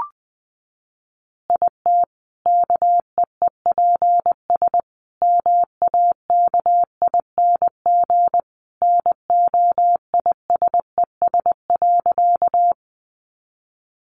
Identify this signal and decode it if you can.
Morse code, heard as a keyed tone, about 20 wpm, 700 Hz. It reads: IT KEEPS MAKING NOISES.
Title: Additional Easy Phrases 20wpm